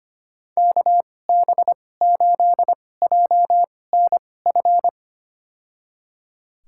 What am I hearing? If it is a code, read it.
Morse code, heard as a keyed tone, about 25 wpm, spaced as for 18 wpm, 700 Hz. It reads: KB8JNF
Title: USA Call Signs - 6 Characters - 2x Character Spacing - 25wpm